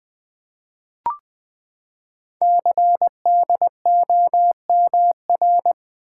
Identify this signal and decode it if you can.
Morse code, heard as a keyed tone, about 20 wpm, 700 Hz. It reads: CDOMR
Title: Code Groups - 5 Letters - 20wpm